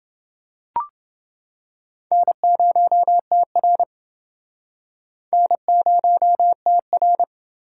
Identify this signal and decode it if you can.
Morse code, heard as a keyed tone, about 29 wpm, 700 Hz. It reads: N0TR N0TR